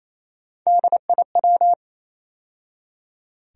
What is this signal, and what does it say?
Morse code, heard as a keyed tone, about 28 wpm, 700 Hz. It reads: DIW
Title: USA Call Sign Postfixes - 3 Letters 28wpm